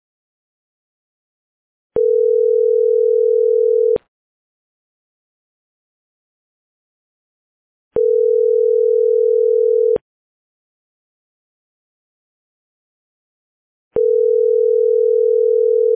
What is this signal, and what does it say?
Ringback tone (North America), 440 + 480 Hz.